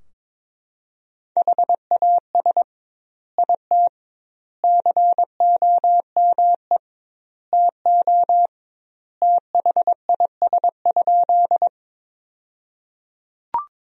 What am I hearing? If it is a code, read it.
Morse code, heard as a keyed tone, about 22 wpm, 700 Hz. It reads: HAS IT COME TO THIS?
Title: 2x Word-Spacing - Sentences from Top 100 Words - 22wpm